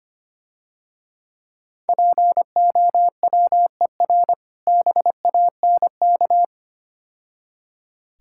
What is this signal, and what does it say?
Morse code, heard as a keyed tone, about 27 wpm, 700 Hz. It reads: POWER BANK
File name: practice / 25wpm / Difficult Words - Speed-Racing with Hints - 25wpm